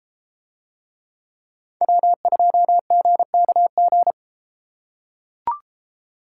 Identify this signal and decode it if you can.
Morse code, heard as a keyed tone, about 32 wpm, 700 Hz. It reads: W2GKG